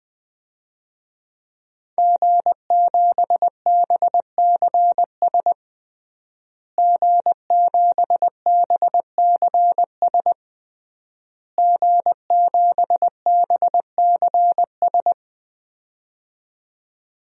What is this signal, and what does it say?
Morse code, heard as a keyed tone, about 20 wpm, 700 Hz. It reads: G7BCS G7BCS G7BCS
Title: DXCC-100 Call Signs - 5 Characters - Repeated 3x - 20wpm